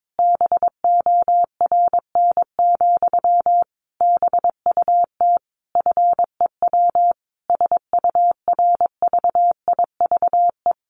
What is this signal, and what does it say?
Morse code, heard as a keyed tone, about 22 wpm, 700 Hz. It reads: BORN, BUT FEW SURVIVE
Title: Sentences from Top 1300 Words 22wpm